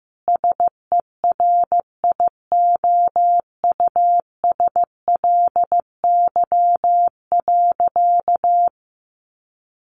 Morse code, heard as a keyed tone, about 15 wpm, 700 Hz. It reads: SERIOUSLY.
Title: Sentences from Top 1200 Words 15wpm